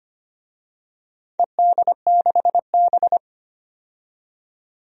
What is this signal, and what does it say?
Morse code, heard as a keyed tone, about 25 wpm, 700 Hz. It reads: ED6B
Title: DXCC-100 Call Signs - 4 Characters - 25wpm